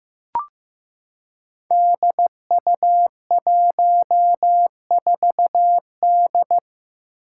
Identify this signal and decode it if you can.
Morse code, heard as a keyed tone, about 15 wpm, 700 Hz. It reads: DU14D
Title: Five Challenging Letter-Number Combinations 15wpm